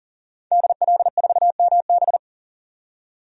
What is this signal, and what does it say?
Morse code, heard as a keyed tone, about 40 wpm, 700 Hz. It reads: DL4MB
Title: DXCC-100 Call Signs - 5 Characters - 40wpm